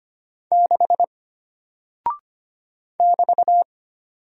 Morse code, heard as a keyed tone, about 25 wpm, 700 Hz. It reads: 6 <BT>
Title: Single Character (Letter:Number:Punctuation:Prosign) Alt 25wpm